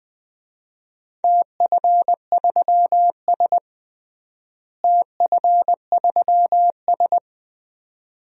Morse code, heard as a keyed tone, about 20 wpm, 700 Hz. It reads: TF3S TF3S
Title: DXCC-100 Call Signs - 4 Characters - Repeated 2x - 20wpm